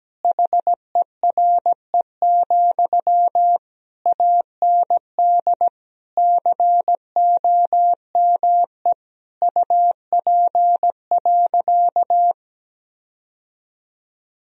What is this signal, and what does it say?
Morse code, heard as a keyed tone, about 17 wpm, 700 Hz. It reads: HERE, AND COME UP.
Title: Sentences from Top 1000 Words - Review Part 2 - 17wpm